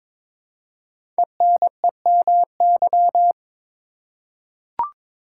Morse code, heard as a keyed tone, about 22 wpm, 700 Hz. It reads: ENEMY
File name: 5 Letter Words - First 3 Letters Given - 22wpm